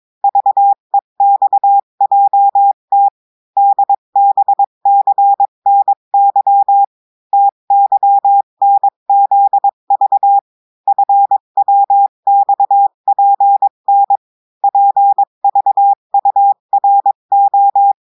Morse code, heard as a keyed tone, about 22 wpm, 800 Hz. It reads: VEXJT DBCNY TYNZV FWXPN PVURO
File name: cours-du-mercredi / audios / cours-853-du-2-mai-2024-etxte chiffre et texte clair-a-20-wpm